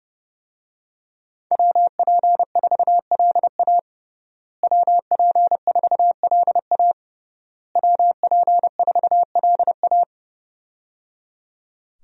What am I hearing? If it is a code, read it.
Morse code, heard as a keyed tone, about 30 wpm, 700 Hz. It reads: WP4LA WP4LA WP4LA
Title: USA Call Signs - 5 Characters - Repeated 3x - 30wpm